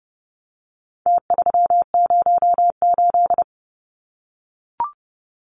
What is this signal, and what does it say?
Morse code, heard as a keyed tone, about 30 wpm, 700 Hz. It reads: T308
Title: Numbers - Contest Style - 4-Digit 30wpm